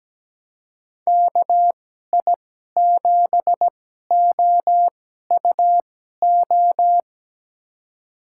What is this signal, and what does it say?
Morse code, heard as a keyed tone, about 17 wpm, spaced as for 12 wpm, 700 Hz. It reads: KI7OUO
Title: USA Call Signs - 2x Character Spacing - 17wpm